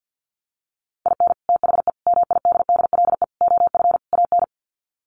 Morse code, heard as a keyed tone, about 50 wpm, 700 Hz. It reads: IN THE MIDDLE OF AN
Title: Sets of 5 Words Encore2 50wpm